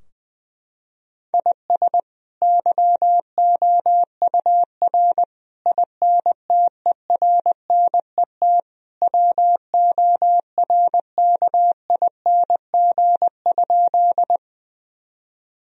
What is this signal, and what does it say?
Morse code, heard as a keyed tone, about 20 wpm, 700 Hz. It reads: IS YOUR INTERNET WORKING?
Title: Sentences from Top 1000 Words - Review Part 4 - 20wpm